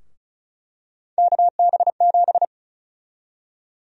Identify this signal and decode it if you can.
Morse code, heard as a keyed tone, about 36 wpm, 700 Hz. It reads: KB7
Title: USA Call Sign Prefixes - Repeated 2x - 35wpm